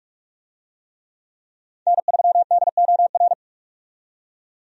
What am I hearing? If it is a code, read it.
Morse code, heard as a keyed tone, about 42 wpm, 700 Hz. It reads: N3DOR
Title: USA Call Signs - 5 Characters - 45wpm